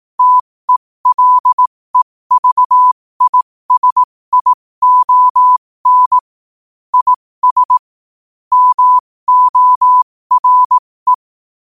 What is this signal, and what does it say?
Morse code, heard as a keyed tone, about 18 wpm, spaced as for 15 wpm, 1000 Hz. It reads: TELEVISION IS MORE
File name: qotdfiles / 15wpm